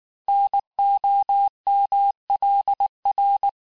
Morse code, heard as a keyed tone, about 19 wpm, 800 Hz. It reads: NOMLR
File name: cours-du-mercredi / audios / cours-869-texte-CHIFFRE-vitesse-18-MPM